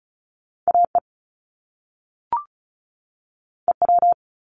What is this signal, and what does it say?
Morse code, heard as a keyed tone, about 36 wpm, 700 Hz. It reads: AE EW
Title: Top Two Letter Combinations in Words 35wpm